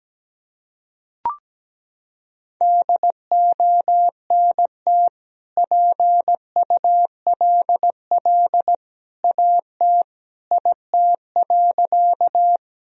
Morse code, heard as a keyed tone, about 17 wpm, 700 Hz. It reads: DONT PULL AT IT.
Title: Sentences from Top 1000 Words - Review Part 2 - 17wpm